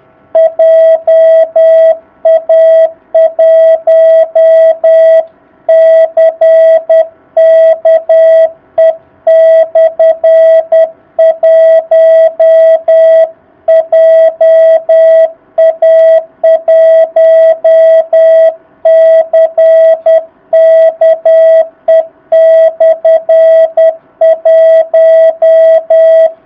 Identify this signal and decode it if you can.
Morse code, heard as a keyed tone, about 10 wpm, 650 Hz. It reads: JA1CKE/1 JA1CKE/1
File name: jh8cbh / 2019cw